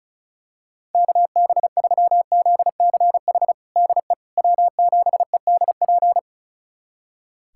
Morse code, heard as a keyed tone, about 34 wpm, 700 Hz. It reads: KB3ZCH DE W7EDP
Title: USA Callsign DE Callsign 35wpm